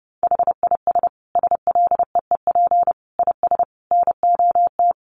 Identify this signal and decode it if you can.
Morse code, heard as a keyed tone, about 30 wpm, 700 Hz. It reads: HIS SLEEP IS NOT
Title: Sentences from Top 700 Words 30wpm